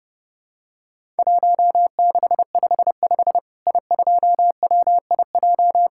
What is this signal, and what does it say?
Morse code, heard as a keyed tone, about 32 wpm, 700 Hz. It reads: 1655 I2WIJ
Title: Worked All Europe DX Contest - QTC Exchange - 30wpm